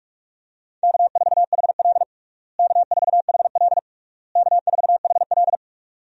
Morse code, heard as a keyed tone, about 46 wpm, 700 Hz. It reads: K4HL K4HL K4HL
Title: USA Call Signs - 4 Characters - Repeated 3x - 45wpm